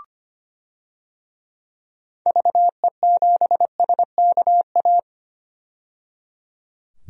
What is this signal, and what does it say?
Morse code, heard as a keyed tone, about 25 wpm, 700 Hz. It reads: VE7SKA
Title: DXCC-100 Call Signs - 6 Characters - 25wpm